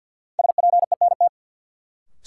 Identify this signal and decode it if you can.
Morse code, heard as a keyed tone, about 46 wpm, 700 Hz. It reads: SPENT
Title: Sets of 5 Words - New Word Drill - 50wpm